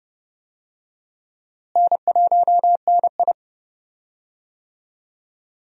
Morse code, heard as a keyed tone, about 30 wpm, 700 Hz. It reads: N1NI
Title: USA Call Signs - 4 Characters - Repeated 2x - 30wpm